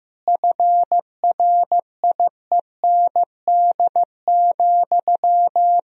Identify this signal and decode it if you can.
Morse code, heard as a keyed tone, about 15 wpm, 700 Hz. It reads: FRIEND,
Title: 3x Word-Spacing - Sentences from Top 500 Words - 15wpm